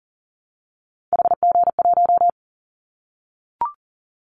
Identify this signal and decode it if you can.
Morse code, heard as a keyed tone, about 40 wpm, 700 Hz. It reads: HG1
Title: Callsign Prefixes for Top 25 DXCC - Repeated 3x - 40wpm